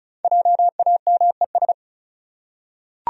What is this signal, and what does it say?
Morse code, heard as a keyed tone, about 34 wpm, 700 Hz. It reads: JAMES
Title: USA Common Names - 35wpm